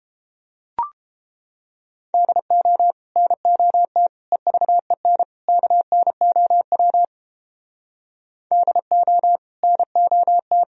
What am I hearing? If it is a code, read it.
Morse code, heard as a keyed tone, about 29 wpm, 700 Hz. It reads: DO NOT EVEN KNOW DO NOT